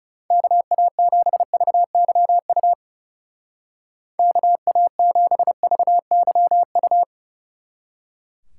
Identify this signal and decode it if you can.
Morse code, heard as a keyed tone, about 38 wpm, 700 Hz. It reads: KA7VYU KA7VYU